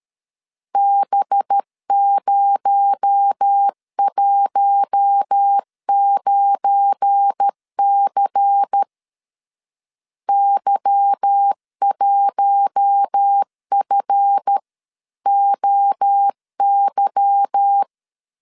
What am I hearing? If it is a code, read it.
Morse code, heard as a keyed tone, about 13 wpm, 800 Hz. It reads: B019C Y1FOY
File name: LEZIONI / 8 YJ45 / 8 Y 12wpm 2228